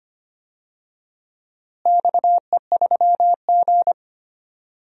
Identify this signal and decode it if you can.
Morse code, heard as a keyed tone, about 25 wpm, 700 Hz. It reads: XE3G
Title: DXCC-100 Call Signs - 4 Characters - Repeated 3x - 25wpm